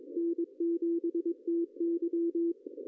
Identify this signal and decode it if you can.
Morse code, heard as a keyed tone, about 21 wpm, 380 Hz. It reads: N7TY
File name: CW / cqww-cw-k5tr